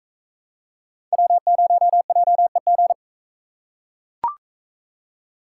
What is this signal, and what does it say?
Morse code, heard as a keyed tone, about 41 wpm, 700 Hz. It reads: W0JEG